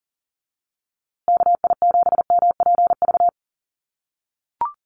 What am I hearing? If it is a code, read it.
Morse code, heard as a keyed tone, about 40 wpm, 700 Hz. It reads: KI7MPV
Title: USA Call Signs Repeated 3x 40wpm